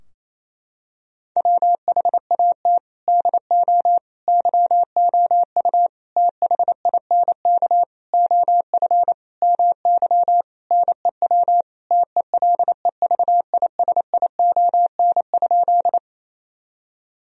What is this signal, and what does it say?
Morse code, heard as a keyed tone, about 28 wpm, 700 Hz. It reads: WHAT DO YOU THINK OF MY NEW TELEVISION?
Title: Sentences from Top 1000 Words - Review Part 4 - 28wpm